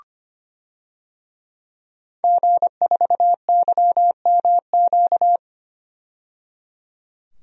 Morse code, heard as a keyed tone, about 25 wpm, 700 Hz. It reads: G4YMQ
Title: DXCC-100 Call Signs - 5 Characters - 25wpm